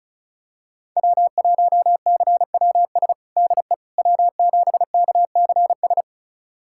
Morse code, heard as a keyed tone, about 34 wpm, 700 Hz. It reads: W1CWS DE W7KCS
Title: USA Callsign DE Callsign - 5 Characters - 35wpm